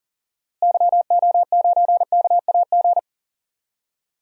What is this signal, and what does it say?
Morse code, heard as a keyed tone, about 40 wpm, 700 Hz. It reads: YO9KAG
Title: DXCC-100 Call Signs - 6 Characters - Repeated 3x - 40wpm